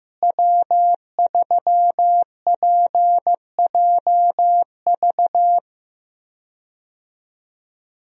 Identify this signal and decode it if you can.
Morse code, heard as a keyed tone, about 15 wpm, 700 Hz. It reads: W3PJV
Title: USA Call Signs - 5 Characters - Repeated 3x - 15wpm